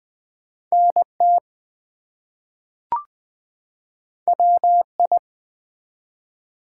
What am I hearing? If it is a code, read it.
Morse code, heard as a keyed tone, about 20 wpm, 700 Hz. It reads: NT WI
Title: US States and Canadian Provinces 20wpm